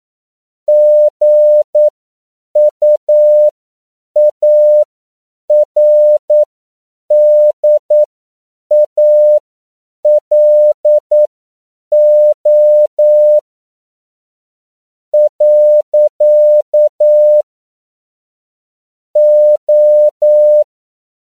Morse code, heard as a keyed tone, about 9 wpm, spaced as for 7 wpm, 600 Hz. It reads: GUARDALO . O
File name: CW / 10PPM